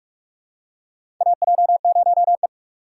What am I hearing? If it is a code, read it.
Morse code, heard as a keyed tone, about 41 wpm, 700 Hz. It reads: AJ0E
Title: USA Call Signs - 4 Characters - 6x Character Spacing - 45wpm